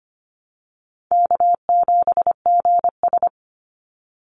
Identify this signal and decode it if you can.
Morse code, heard as a keyed tone, about 25 wpm, 700 Hz. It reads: K7GS
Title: Easy Call Signs Repeated 3x - 25wpm